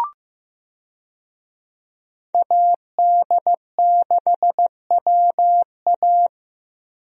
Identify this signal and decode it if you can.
Morse code, heard as a keyed tone, about 15 wpm, 700 Hz. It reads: AD6WA
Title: USA Call Signs - 5 Characters - Repeated 3x - 15wpm